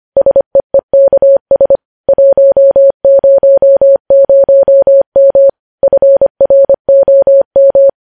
Morse code, heard as a keyed tone, about 25 wpm, 550 Hz. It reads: SEEKS 100M FROM
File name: podcast / News headlines-1x-25x25